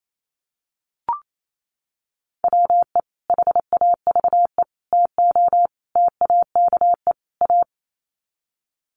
Morse code, heard as a keyed tone, about 28 wpm, 700 Hz. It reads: WE HAVE TO TAKE A
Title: Sets of 5 Words Encore 28wpm